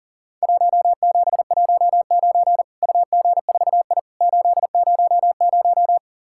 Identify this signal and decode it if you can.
Morse code, heard as a keyed tone, about 40 wpm, 700 Hz. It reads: 1719 UG4I 800
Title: Worked All Europe DX Contest - QTC Exchange - 40wpm